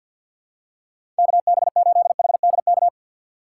Morse code, heard as a keyed tone, about 51 wpm, 700 Hz. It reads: KB8HDK
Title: USA Call Signs - 6x Character Spacing - 50wpm